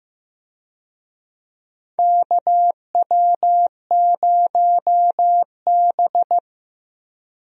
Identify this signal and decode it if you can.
Morse code, heard as a keyed tone, about 15 wpm, 700 Hz. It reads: KW0B
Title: USA Call Signs - 4 Characters - Repeated 3x - 15wpm